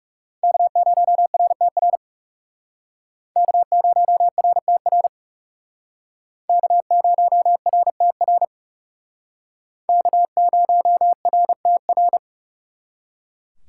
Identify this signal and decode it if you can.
Morse code, heard as a keyed tone, about 37 wpm, 700 Hz. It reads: K0RTR K0RTR K0RTR K0RTR
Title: Speed Racing - USA 1x3 Call Signs 30wpm